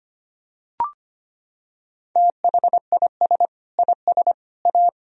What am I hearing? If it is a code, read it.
Morse code, heard as a keyed tone, about 25 wpm, 700 Hz. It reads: THIS IS A
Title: Sentences from Top 1000 Words - Review Part 4 - 25wpm